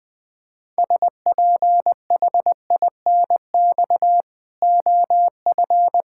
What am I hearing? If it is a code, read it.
Morse code, heard as a keyed tone, about 20 wpm, 700 Hz. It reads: SPHINX OF
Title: Pangram Challenge v2 - 20wpm